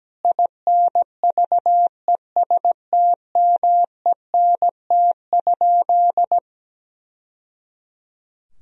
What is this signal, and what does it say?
Morse code, heard as a keyed tone, about 17 wpm, 700 Hz. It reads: INVESTMENT?
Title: Sentences from Top 1000 Words - Review Part 4 - 17wpm